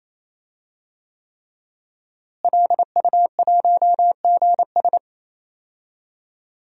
Morse code, heard as a keyed tone, about 28 wpm, 700 Hz. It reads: LU1GS
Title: DXCC-100 Call Signs - 5 Characters - Repeated 3x - 28wpm